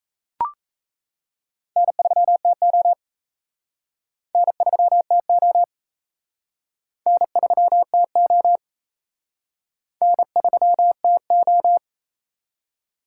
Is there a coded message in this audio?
Yes: Morse code from 1.8 to 11.8 s, heard as a keyed tone, about 37 wpm, 700 Hz. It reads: N3TO N3TO N3TO N3TO